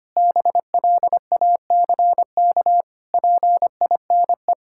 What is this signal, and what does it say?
Morse code, heard as a keyed tone, about 25 wpm, 700 Hz. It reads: BLACK PINE
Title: Pangram Challenge v2 - 25wpm